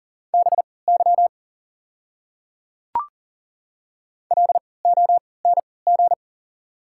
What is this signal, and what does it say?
Morse code, heard as a keyed tone, about 38 wpm, spaced as for 22 wpm, 700 Hz. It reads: BY LONG
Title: 3x Character Spacing - Top 100 Words - 40wpm